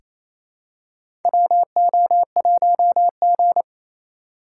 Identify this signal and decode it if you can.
Morse code, heard as a keyed tone, about 28 wpm, 700 Hz. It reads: WO1G
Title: USA Call Signs - 4 Characters - 6x Character Spacing - 28wpm